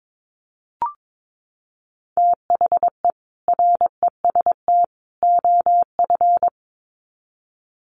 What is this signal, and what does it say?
Morse code, heard as a keyed tone, about 22 wpm, 700 Hz. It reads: THE REST OF
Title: Sets of 3 Words 22wpm